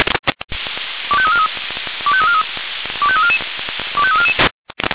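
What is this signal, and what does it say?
Five-tone selective call (ZVEI), heard as quick runs of five tones, tones of about 70 ms. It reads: 25343 25343 25344 25344
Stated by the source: ZVEI five-tone selective call 25212, received by SDR.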